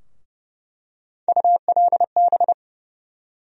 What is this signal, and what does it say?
Morse code, heard as a keyed tone, about 30 wpm, 700 Hz. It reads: ULB
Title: USA Call Sign Postfixes - 3 Letters 30wpm